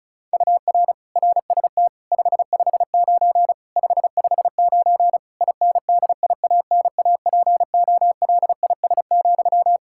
Morse code, heard as a keyed tone, about 34 wpm, 700 Hz. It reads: UR RST 559 559 INDIANAPOLIS,
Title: Practice QSO - Exchange 1 - Beginning - 35wpm